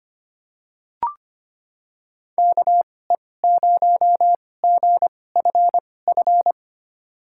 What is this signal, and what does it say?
Morse code, heard as a keyed tone, about 25 wpm, spaced as for 18 wpm, 700 Hz. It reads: KE0GFF
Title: USA Call Signs - 6 Characters - 2x Character Spacing - 25wpm